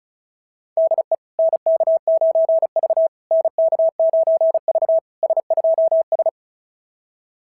Morse code, heard as a keyed tone, about 36 wpm, 690 Hz. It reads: DE NK9V NK9V S2S
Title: SOTA QSO Practice 35wpm